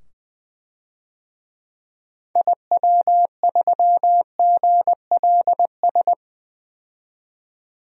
Morse code, heard as a keyed tone, about 20 wpm, 700 Hz. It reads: IW3GLS
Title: DXCC-100 Call Signs - 6 Characters - 20wpm